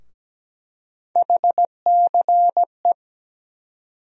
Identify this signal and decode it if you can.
Morse code, heard as a keyed tone, about 17 wpm, 700 Hz. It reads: HCE
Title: USA Call Sign Postfixes - 3 Letters 17wpm